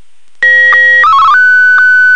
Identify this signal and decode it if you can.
SSTV (slow-scan television), Martin 1.